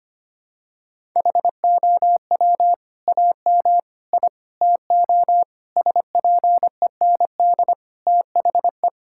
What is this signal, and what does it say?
Morse code, heard as a keyed tone, about 25 wpm, 700 Hz. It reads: HOW AM I TO SPEND THE
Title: Sentences from Top 1000 Words - Review Part 5 - 25wpm